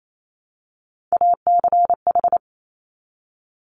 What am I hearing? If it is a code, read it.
Morse code, heard as a keyed tone, about 28 wpm, 690 Hz. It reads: ACH
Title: Top Three Letter Combinations in Words 28wpm